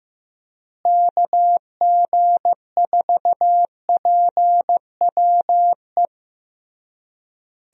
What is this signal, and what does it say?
Morse code, heard as a keyed tone, about 15 wpm, 700 Hz. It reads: KG4PWE
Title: USA Call Signs - 6 Characters - Repeated 3x - 15wpm